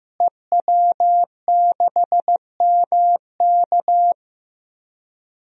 Morse code, heard as a keyed tone, about 15 wpm, 700 Hz. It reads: EW6MK
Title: DXCC-100 Call Signs - 5 Characters - Repeated 3x - 15wpm